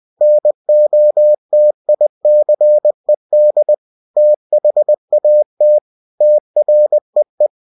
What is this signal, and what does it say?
Morse code, heard as a keyed tone, about 20 wpm, 600 Hz. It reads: NOTICED THAT TREE
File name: cw-books / winnie-the-pooh / winnie-20wpm-page-0068